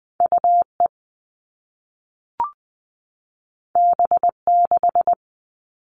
Morse code, heard as a keyed tone, about 20 wpm, 700 Hz. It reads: UE B6
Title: All Cut Numbers - 2-Digit 20wpm